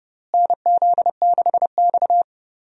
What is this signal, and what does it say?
Morse code, heard as a keyed tone, about 30 wpm, 700 Hz. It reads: NZ6X
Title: USA Call Signs - 4 Characters - Repeated 2x - 30wpm